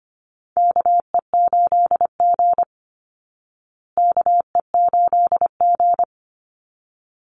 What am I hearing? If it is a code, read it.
Morse code, heard as a keyed tone, about 25 wpm, 700 Hz. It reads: KE8G KE8G
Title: Call Signs Repeated 3x - 25wpm